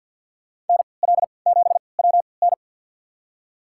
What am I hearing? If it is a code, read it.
Morse code, heard as a keyed tone, about 47 wpm, spaced as for 28 wpm, 700 Hz. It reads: NL7WN
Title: USA Call Signs - 3x Character Spacing - 50wpm